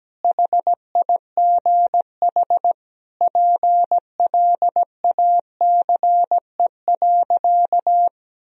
Morse code, heard as a keyed tone, about 17 wpm, 700 Hz. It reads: HIGH PLACE.
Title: Sentences from Top 1000 Words - Review Part 1 - 17wpm